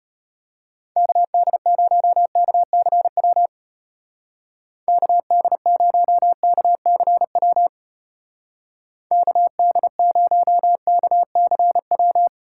Morse code, heard as a keyed tone, about 31 wpm, 700 Hz. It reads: KD0KCW KD0KCW KD0KCW
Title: Speed Racing - USA 2x3 Call Signs 25wpm